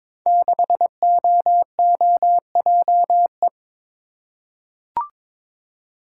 Morse code, heard as a keyed tone, about 22 wpm, 700 Hz. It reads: 6OOJE
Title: Code Groups - 5 Letter:Numbers - 22wpm